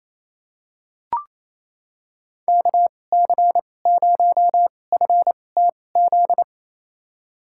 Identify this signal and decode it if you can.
Morse code, heard as a keyed tone, about 28 wpm, spaced as for 20 wpm, 700 Hz. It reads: KC0FTZ
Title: USA Call Signs - 6 Characters - 2x Character Spacing - 28wpm